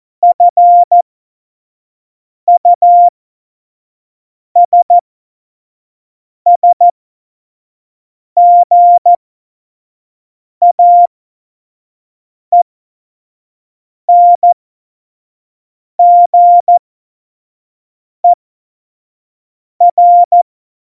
Morse code, heard as a keyed tone, about 14 wpm, spaced as for 5 wpm, 700 Hz. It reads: FUSSGAENGER
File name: Morsekurs / Witze / Witz 02 Juni 2020, 25 F70